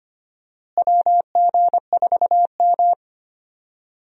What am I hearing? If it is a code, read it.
Morse code, heard as a keyed tone, about 25 wpm, 700 Hz. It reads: WG4M
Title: USA Call Signs - 4 Characters - Repeated 3x - 25wpm